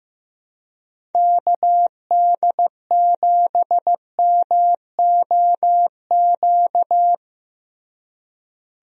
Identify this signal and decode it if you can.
Morse code, heard as a keyed tone, about 15 wpm, 700 Hz. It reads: KD7MOQ
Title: USA Call Signs - 6 Characters - Repeated 3x - 15wpm